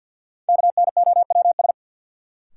Morse code, heard as a keyed tone, about 52 wpm, 700 Hz. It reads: KNOWS
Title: Sentences from Top 1000 Words - Review Part 1 - New Word Drill - 50wpm